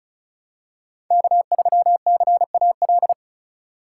Morse code, heard as a keyed tone, about 34 wpm, 700 Hz. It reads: K3CAL
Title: USA Call Signs - 3x Character Spacing - 35wpm